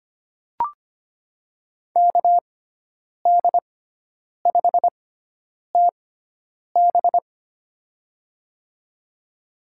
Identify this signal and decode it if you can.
Morse code, heard as a keyed tone, about 25 wpm, spaced as for 9 wpm, 700 Hz. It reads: KD5TB
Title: USA Call Signs - 5 Characters - 6x Character Spacing - 25wpm